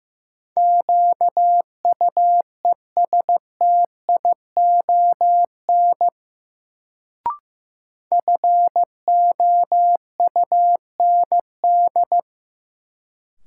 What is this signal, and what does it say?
Morse code, heard as a keyed tone, about 16 wpm, 700 Hz. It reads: QUESTION FOUND
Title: Sets of 5 Words Encore - New Word Drill - 15wpm